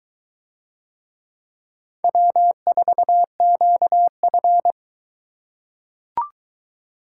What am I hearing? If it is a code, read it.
Morse code, heard as a keyed tone, about 21 wpm, 700 Hz. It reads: W4QF